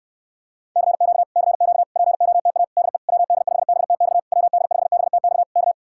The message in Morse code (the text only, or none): CQ CQ POTA DE WD5BTX WD5BTX K